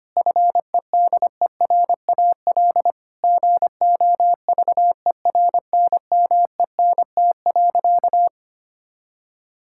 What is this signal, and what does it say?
Morse code, heard as a keyed tone, about 25 wpm, 700 Hz. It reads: FEDERAL GOVERNMENT.
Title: Sentences from Top 1000 Words - Review Part 5 - 25wpm